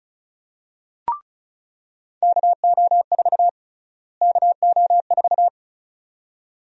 Morse code, heard as a keyed tone, about 36 wpm, 700 Hz. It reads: KO4 KO4
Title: USA Call Sign Prefixes - Repeated 2x - 35wpm